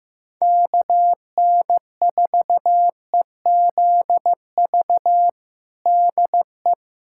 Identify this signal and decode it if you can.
Morse code, heard as a keyed tone, about 15 wpm, 700 Hz. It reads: KN4EZV DE
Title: USA Callsign DE Callsign 15wpm